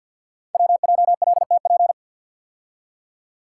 Morse code, heard as a keyed tone, about 52 wpm, 700 Hz. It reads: WJLTP
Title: Code Groups - 5 Letters - 50wpm